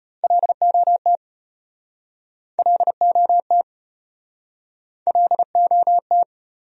Morse code, heard as a keyed tone, about 36 wpm, 700 Hz. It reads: LOT LOT LOT